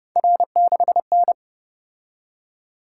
Morse code, heard as a keyed tone, about 30 wpm, 700 Hz. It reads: R6N
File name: Speed-Racing - Three Letter-Number Combinations - 25wpm